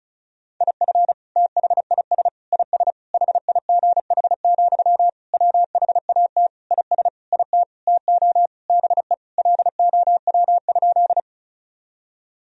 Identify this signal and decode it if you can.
Morse code, heard as a keyed tone, about 34 wpm, 700 Hz. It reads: IF THIS IS HIGH, WHAT IS IT TO BE LOW?
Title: Sentences from Top 1000 Words - Review Part 1 - 35wpm